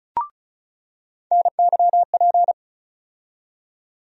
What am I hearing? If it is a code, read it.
Morse code, heard as a keyed tone, about 35 wpm, 700 Hz. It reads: NYP
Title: USA Call Sign Postfixes - 3 Letters 35wpm